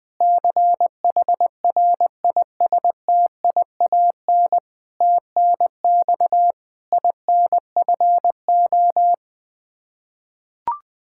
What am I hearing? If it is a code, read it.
Morse code, heard as a keyed tone, about 20 wpm, 700 Hz. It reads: CHRISTIAN TNX INFO
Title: K1USN SST QSO Practice 20wpm